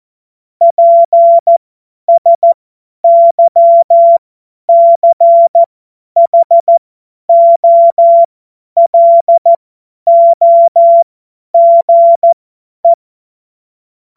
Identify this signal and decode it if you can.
Morse code, heard as a keyed tone, about 14 wpm, spaced as for 10 wpm, 680 Hz. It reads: PSYCHOLOGE
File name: Morsekurs / Witze / Witz 02 April 2024 50 F70